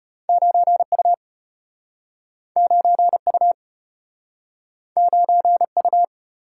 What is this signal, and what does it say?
Morse code, heard as a keyed tone, about 34 wpm, 700 Hz. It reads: 9U 9U 9U